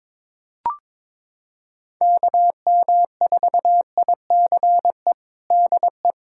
Morse code, heard as a keyed tone, about 22 wpm, 700 Hz. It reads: KM4ICE DE
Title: USA Callsign DE Callsign - 6 Characters - 22wpm